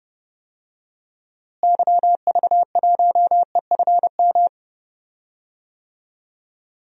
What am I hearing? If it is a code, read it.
Morse code, heard as a keyed tone, about 31 wpm, 700 Hz. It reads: YV1EFM
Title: DXCC-100 Call Signs - 6 Characters - Repeated 3x - 30wpm